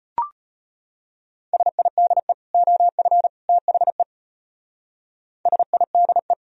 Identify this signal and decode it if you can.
Morse code, heard as a keyed tone, about 31 wpm, 700 Hz. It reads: SIDE OF THE SIDE
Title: Speed-Racing - Sets of 3 Words 25wpm